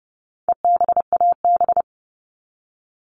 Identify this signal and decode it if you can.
Morse code, heard as a keyed tone, about 30 wpm, 700 Hz. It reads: EBAB
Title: All Cut Numbers - 4-Digit 30wpm